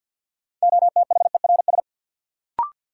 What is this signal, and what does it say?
Morse code, heard as a keyed tone, about 53 wpm, 700 Hz. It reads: OTHERS
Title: Sentences from Top 1000 Words - Review Part 2 - New Word Drill - 50wpm